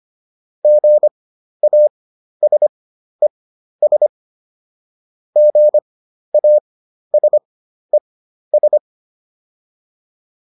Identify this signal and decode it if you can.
Morse code, heard as a keyed tone, about 25 wpm, spaced as for 12 wpm, 600 Hz. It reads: GASES GASES